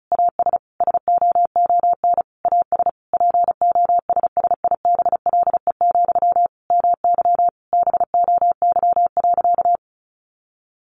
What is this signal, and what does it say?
Morse code, heard as a keyed tone, about 34 wpm, 700 Hz. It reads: AS SOON AS POSSIBLE, MY BOY.
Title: Sentences from Top 800 Words 35wpm